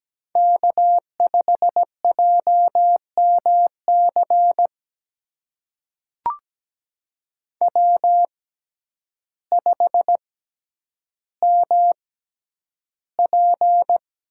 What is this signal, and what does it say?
Morse code, heard as a keyed tone, about 17 wpm, spaced as for 9 wpm, 700 Hz. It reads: K5JMC W5MP
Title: USA Call Signs - 6x Character Spacing - 17wpm